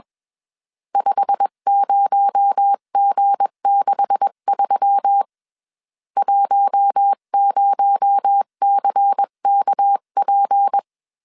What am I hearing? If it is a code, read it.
Morse code, heard as a keyed tone, about 21 wpm, 800 Hz. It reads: 50G63 10CKP
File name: CW / DET / 1 9 1500C S B 20wpm 2313